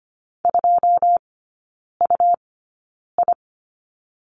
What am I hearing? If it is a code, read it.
Morse code, heard as a keyed tone, about 25 wpm, spaced as for 9 wpm, 700 Hz. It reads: 2UI
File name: Lesson 17e - Letter u - Three Characters 25wpm